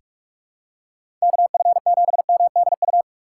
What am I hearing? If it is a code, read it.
Morse code, heard as a keyed tone, about 42 wpm, 700 Hz. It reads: KF7MDU
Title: USA Call Signs - 6 Characters - 4x Character Spacing - 45wpm